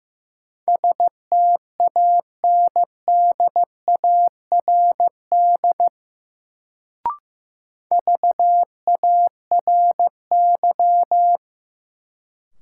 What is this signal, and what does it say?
Morse code, heard as a keyed tone, about 15 wpm, 700 Hz. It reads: STANDARD VARY
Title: Sentences from Top 1200 Words - New Word Drill Alt - 15wpm